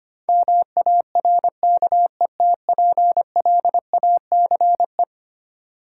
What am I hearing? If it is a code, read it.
Morse code, heard as a keyed tone, about 25 wpm, 700 Hz. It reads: MARKETPLACE
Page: morse-code-ninja-practice-25wpm